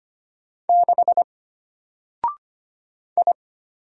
Morse code, heard as a keyed tone, about 25 wpm, 700 Hz. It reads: 6 I
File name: Single-Letter-Number-25wpm